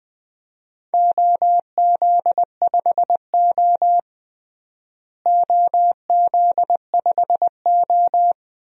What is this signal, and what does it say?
Morse code, heard as a keyed tone, about 20 wpm, 700 Hz. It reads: OZ5O OZ5O
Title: DXCC-100 Call Signs - 4 Characters - Repeated 2x - 20wpm